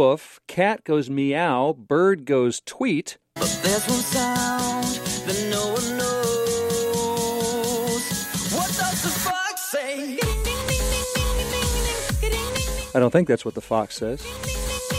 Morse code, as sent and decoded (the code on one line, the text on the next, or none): none